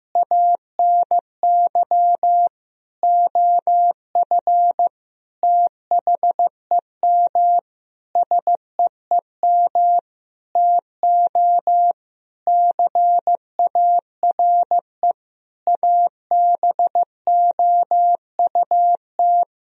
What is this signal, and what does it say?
Morse code, heard as a keyed tone, about 16 wpm, 700 Hz. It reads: ANY OF THEM SEEM TO CARE ABOUT
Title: Sentences from Top 1000 Words - Review Part 3 - 15wpm